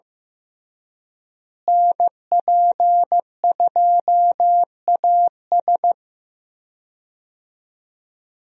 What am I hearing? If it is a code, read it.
Morse code, heard as a keyed tone, about 14 wpm, 700 Hz. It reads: NP2AS
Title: USA Call Signs - 5 Characters - Repeated 3x - 15wpm